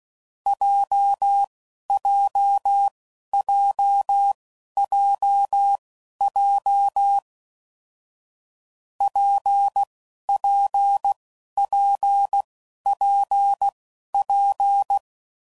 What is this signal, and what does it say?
Morse code, heard as a keyed tone, about 16 wpm, spaced as for 11 wpm, 800 Hz. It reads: JJJJJ PPPPP
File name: Combinations / Difficult Combos / WPJ WPJ1